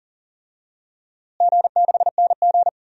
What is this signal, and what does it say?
Morse code, heard as a keyed tone, about 40 wpm, 700 Hz. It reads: G6NG
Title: DXCC-100 Call Signs - 4 Characters - Repeated 2x - 40wpm